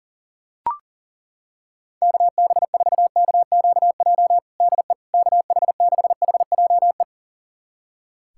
Morse code, heard as a keyed tone, about 40 wpm, 700 Hz. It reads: KB4KQJ DE KH6HJE
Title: USA Callsign DE Callsign 40wpm